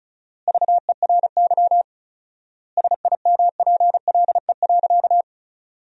Morse code, heard as a keyed tone, about 36 wpm, 700 Hz. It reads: VERY SIMPLE.
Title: 4x Word-Spacing - Sentences from Top 500 Words - 35wpm